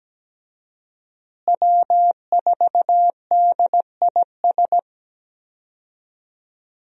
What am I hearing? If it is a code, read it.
Morse code, heard as a keyed tone, about 17 wpm, 700 Hz. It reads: W4DIS
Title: USA Call Signs - 5 Characters - Repeated 3x - 17wpm